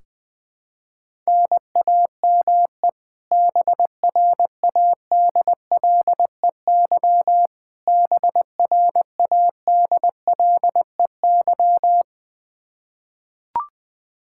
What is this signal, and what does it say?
Morse code, heard as a keyed tone, about 20 wpm, 700 Hz. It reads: NAME BRADLEY BRADLEY
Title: Practice QSO - Exchange 1 - Beginning - 20wpm